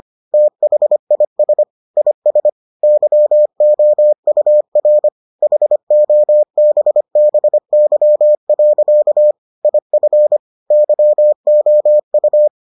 Morse code, heard as a keyed tone, about 25 wpm, 600 Hz. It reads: THIS IS YOUR HOBBY. IF YOU